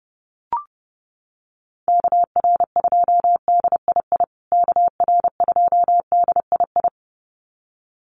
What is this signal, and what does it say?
Morse code, heard as a keyed tone, about 30 wpm, 700 Hz. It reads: KR2DII KR2DII
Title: USA Call Signs Repeated 2x 30wpm